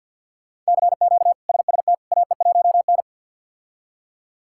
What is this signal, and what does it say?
Morse code, heard as a keyed tone, about 50 wpm, 700 Hz. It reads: CQ SST AE1N
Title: K1USN SST QSO Practice - 50wpm